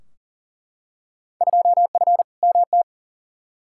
Morse code, heard as a keyed tone, about 40 wpm, 700 Hz. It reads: 2F MT
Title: Field Day Exchanges Encore - 40wpm